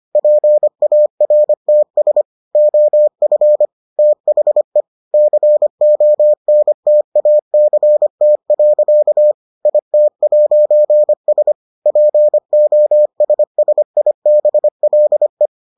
Morse code, heard as a keyed tone, about 25 wpm, 600 Hz. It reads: PARTS OF THE CONTACT. IT'S POSSIBLE